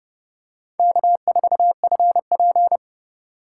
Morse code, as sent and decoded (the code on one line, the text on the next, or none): -.- ....- ..-. .--.
K4FP